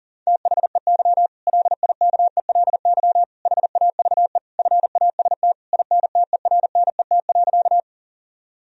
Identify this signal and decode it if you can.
Morse code, heard as a keyed tone, about 37 wpm, 700 Hz. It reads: THEY LIKELY HAVE FAST INTERNET.
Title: Sentences from Top 1000 Words - Review Part 4 - 40wpm